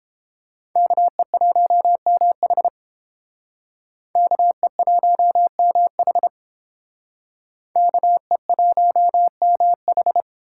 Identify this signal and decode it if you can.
Morse code, heard as a keyed tone, about 31 wpm, 700 Hz. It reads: KE1MH KE1MH KE1MH